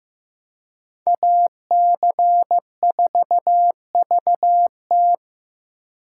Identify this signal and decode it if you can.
Morse code, heard as a keyed tone, about 18 wpm, 700 Hz. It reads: AC4VT
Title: Speed Racing - USA 2x2 Call Signs 15wpm